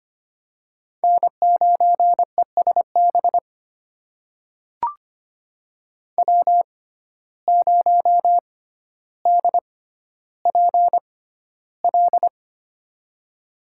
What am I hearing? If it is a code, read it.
Morse code, heard as a keyed tone, about 25 wpm, spaced as for 13 wpm, 700 Hz. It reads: N9ESB W0DPL
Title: USA Call Signs - 5 Characters - 6x Character Spacing - 25wpm